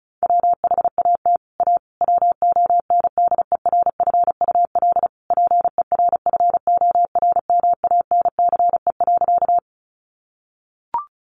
Morse code, heard as a keyed tone, about 34 wpm, 700 Hz. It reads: WHAT A WONDERFUL PERFORMANCE.
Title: Sentences from Top 1100 Words 35wpm